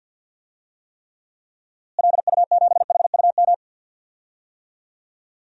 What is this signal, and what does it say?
Morse code, heard as a keyed tone, about 50 wpm, 700 Hz. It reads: LU7HUM
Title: DXCC-100 Call Signs - 6 Characters - Repeated 2x - 50wpm